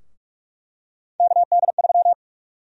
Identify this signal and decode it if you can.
Morse code, heard as a keyed tone, about 46 wpm, 700 Hz. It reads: KD3
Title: USA Call Sign Prefixes - Repeated 2x - 45wpm